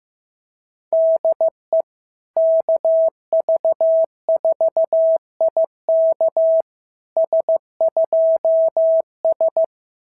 Morse code, heard as a keyed tone, about 15 wpm, 690 Hz. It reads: DE KV4IK S2S
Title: SOTA QSO Practice 15wpm